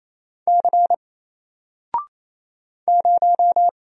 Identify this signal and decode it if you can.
Morse code, heard as a keyed tone, about 27 wpm, 700 Hz. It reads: C 0
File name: Single-Letter-Number-28wpm